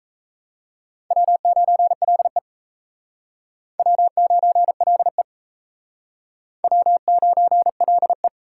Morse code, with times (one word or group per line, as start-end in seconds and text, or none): 1.1-2.4 W9LE
3.8-5.2 W9LE
6.6-8.3 W9LE